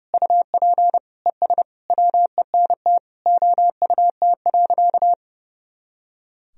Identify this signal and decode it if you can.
Morse code, heard as a keyed tone, about 32 wpm, 700 Hz. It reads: UP ES WENT OUT.
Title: Sentences from Top 300 Words and Abbreviations 30wpm